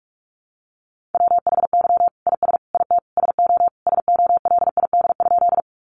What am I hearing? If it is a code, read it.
Morse code, heard as a keyed tone, about 44 wpm, 700 Hz. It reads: WHY IS IT SO SOLID?
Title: Sentences from Top 1500 Words 45wpm